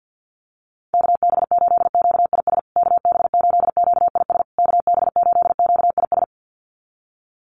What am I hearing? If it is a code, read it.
Morse code, heard as a keyed tone, about 50 wpm, 700 Hz. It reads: KB8QIS KB8QIS KB8QIS
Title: USA Call Signs Repeated 3x 50wpm